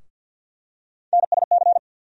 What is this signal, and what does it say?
Morse code, heard as a keyed tone, about 52 wpm, 690 Hz. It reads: NSC
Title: USA Call Sign Postfixes - 3 Letters 50wpm